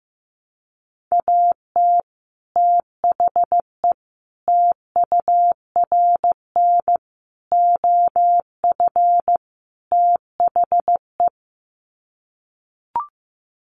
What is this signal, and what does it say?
Morse code, heard as a keyed tone, about 15 wpm, 700 Hz. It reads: AT THE TURN OF THE
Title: Sets of 5 Words 15wpm